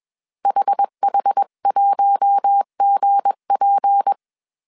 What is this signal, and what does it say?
Morse code, heard as a keyed tone, about 21 wpm, 800 Hz. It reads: HH1GP